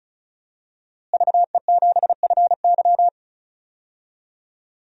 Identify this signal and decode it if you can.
Morse code, heard as a keyed tone, about 36 wpm, 700 Hz. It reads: VE7FY
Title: DXCC-100 Call Signs - 5 Characters - Repeated 3x - 35wpm